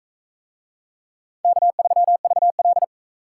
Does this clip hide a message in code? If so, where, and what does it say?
Yes: Morse code from 1.4 to 2.8 s, heard as a keyed tone, about 42 wpm, 700 Hz. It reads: K3VL